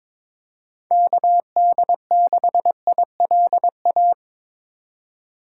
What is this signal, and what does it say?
Morse code, heard as a keyed tone, about 26 wpm, 700 Hz. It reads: KD6ILA
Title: Speed Racing - USA 2x3 Call Signs 22wpm